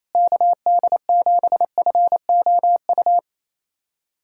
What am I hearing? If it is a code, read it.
Morse code, heard as a keyed tone, about 27 wpm, 700 Hz. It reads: KD7FOU
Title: USA Callsign DE Callsign 28wpm